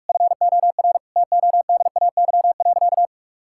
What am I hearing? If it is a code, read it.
Morse code, heard as a keyed tone, about 46 wpm, 700 Hz. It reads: FOR TODAY.
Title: Sentences from Top 1000 Words - Review Part 5 - 45wpm